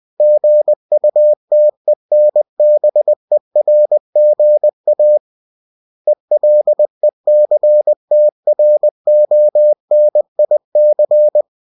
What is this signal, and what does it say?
Morse code, heard as a keyed tone, about 20 wpm, 600 Hz. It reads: GUTENBERGA ELECTRONIC